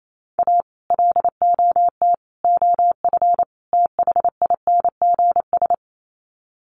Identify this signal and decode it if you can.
Morse code, heard as a keyed tone, about 28 wpm, 700 Hz. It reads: A LOT OF THINGS
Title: Sets of 4 Words 28wpm